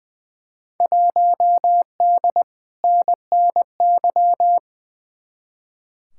Morse code, heard as a keyed tone, about 21 wpm, 700 Hz. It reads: 1D NNY
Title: Field Day Exchanges Encore - 20wpm